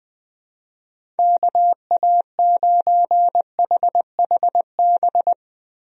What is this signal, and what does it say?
Morse code, heard as a keyed tone, about 20 wpm, 700 Hz. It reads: KA9HHB
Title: USA Call Signs - 6 Characters - Repeated 3x - 20wpm